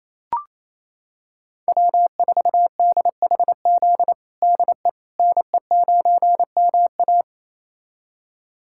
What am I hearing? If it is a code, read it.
Morse code, heard as a keyed tone, about 27 wpm, 700 Hz. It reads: W4DHZ DE NE9MA
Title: USA Callsign DE Callsign - 5 Characters - 28wpm